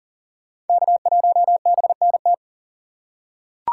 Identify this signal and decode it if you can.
Morse code, heard as a keyed tone, about 40 wpm, 700 Hz. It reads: K1BNT
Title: USA Call Signs - 5 Characters - 4x Character Spacing - 40wpm